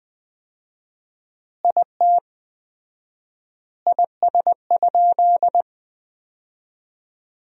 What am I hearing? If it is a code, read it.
Morse code, heard as a keyed tone, about 20 wpm, 700 Hz. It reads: IT IS?
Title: 4x Word-Spacing - Sentences from Top 100 Words - 20wpm